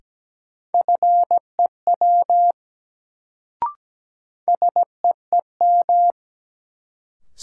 Morse code, heard as a keyed tone, about 17 wpm, 700 Hz. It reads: FEW SEEM
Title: Sentences from Top 300 Words - New Word Drill - 17wpm